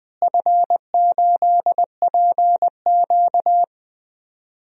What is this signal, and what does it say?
Morse code, heard as a keyed tone, about 20 wpm, 700 Hz. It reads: F8PQ
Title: DXCC-100 Call Signs - 4 Characters - Repeated 3x - 20wpm